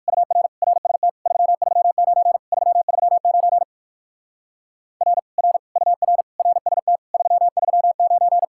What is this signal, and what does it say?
Morse code, heard as a keyed tone, about 58 wpm, 700 Hz. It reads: UR RST 339 339 R R UR RST 339